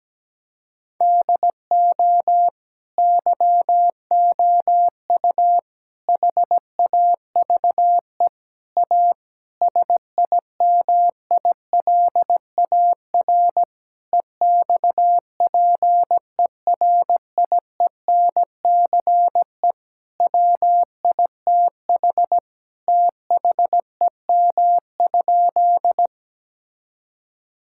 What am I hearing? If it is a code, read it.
Morse code, heard as a keyed tone, about 17 wpm, 700 Hz. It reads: DO YOU HAVE A SIMILAR EXPERIENCE WITH THEM?
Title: Sentences from Top 1000 Words - Review Part 5 - 17wpm